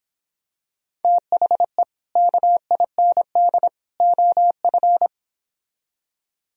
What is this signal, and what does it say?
Morse code, heard as a keyed tone, about 26 wpm, 700 Hz. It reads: THE KIND OF